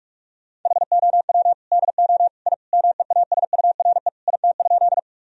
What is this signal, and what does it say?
Morse code, heard as a keyed tone, about 46 wpm, 700 Hz. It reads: HOW DO I MEASURE IT?
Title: Sentences from Top 1000 Words - Review Part 2 - 45wpm